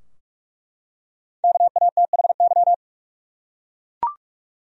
Morse code, heard as a keyed tone, about 45 wpm, 700 Hz. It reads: KATHY
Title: USA Common Names - Max 5 Letters - 45wpm